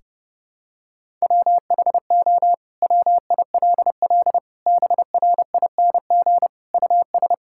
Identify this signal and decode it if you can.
Morse code, heard as a keyed tone, about 30 wpm, 700 Hz. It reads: WHO WILL BRING US
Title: Sentences from Top 1600 Words 30wpm